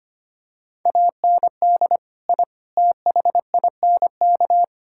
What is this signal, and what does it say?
Morse code, heard as a keyed tone, about 25 wpm, 700 Hz. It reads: AND I THINK
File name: practice / 25wpm / Sentences with 3-Word Phrases - 25wpm